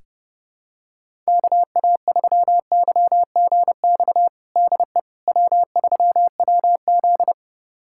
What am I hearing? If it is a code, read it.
Morse code, heard as a keyed tone, about 30 wpm, 700 Hz. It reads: KA3YGX DE W3WZ
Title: USA Callsign DE Callsign 30wpm